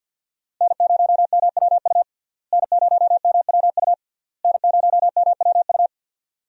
Morse code, heard as a keyed tone, about 50 wpm, 700 Hz. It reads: N0MWU N0MWU N0MWU
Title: USA Call Signs - 5 Characters - Repeated 3x - 50wpm